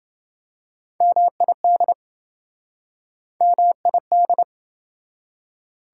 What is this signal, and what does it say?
Morse code, heard as a keyed tone, about 28 wpm, 700 Hz. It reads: MID MID